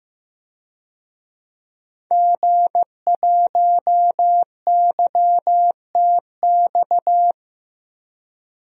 Morse code, heard as a keyed tone, about 15 wpm, 700 Hz. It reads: G1YTX
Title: DXCC-100 Call Signs - 5 Characters - 15wpm